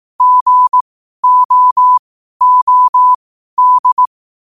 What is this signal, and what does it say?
Morse code, heard as a keyed tone, about 18 wpm, spaced as for 12 wpm, 1000 Hz. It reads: GOOD
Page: qotd-at-12-wpm